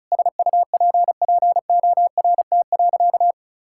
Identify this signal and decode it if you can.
Morse code, heard as a keyed tone, about 36 wpm, 700 Hz. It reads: SUPPORT.
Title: Sentences from Top 1000 Words - Review Part 5 - 35wpm